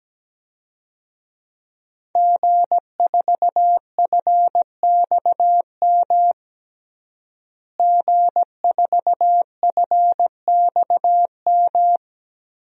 Morse code, heard as a keyed tone, about 17 wpm, 700 Hz. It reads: G4FXM G4FXM
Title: DXCC-100 Call Signs - 5 Characters - Repeated 2x - 17wpm